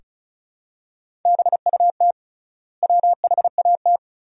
Morse code, heard as a keyed tone, about 36 wpm, 700 Hz. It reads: BUT WHAT
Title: 3x Word-Spacing - Sentences from Top 100 Words - 35wpm